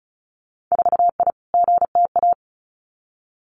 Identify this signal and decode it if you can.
Morse code, heard as a keyed tone, about 36 wpm, 700 Hz. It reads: 4I GTA
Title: Winter Field Day Exchanges Encore 35wpm